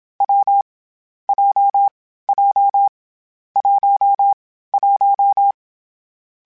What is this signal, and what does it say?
Morse code, heard as a keyed tone, about 27 wpm, 800 Hz. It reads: W J J 1 1
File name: Difficult Combos / Difficult Combos-mp3s / AWJ1 @25